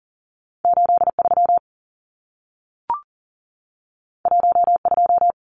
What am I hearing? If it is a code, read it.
Morse code, heard as a keyed tone, about 40 wpm, 700 Hz. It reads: 83 12
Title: Numbers - 2-Digit 40wpm